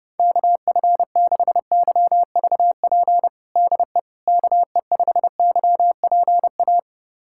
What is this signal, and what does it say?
Morse code, heard as a keyed tone, about 32 wpm, 700 Hz. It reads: KF6YVP DE KE5YPA
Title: USA Callsign DE Callsign 30wpm